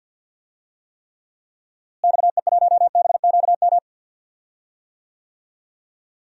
Morse code, heard as a keyed tone, about 53 wpm, 700 Hz. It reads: XE1BQM
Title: DXCC-100 Call Signs - 6 Characters - Repeated 2x - 50wpm